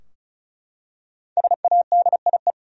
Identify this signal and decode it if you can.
Morse code, heard as a keyed tone, about 34 wpm, 700 Hz. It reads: SADIE
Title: USA Common Names - 35wpm